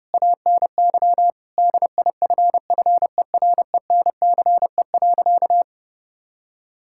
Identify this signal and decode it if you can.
Morse code, heard as a keyed tone, about 30 wpm, 700 Hz. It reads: ANY DIFFERENCE.
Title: Sentences from Top 1000 Words - Review Part 5 - 30wpm